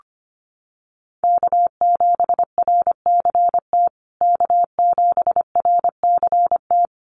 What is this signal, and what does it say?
Morse code, heard as a keyed tone, about 25 wpm, 700 Hz. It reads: K7RCT K7RCT
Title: USA Call Signs Repeated 3x 25wpm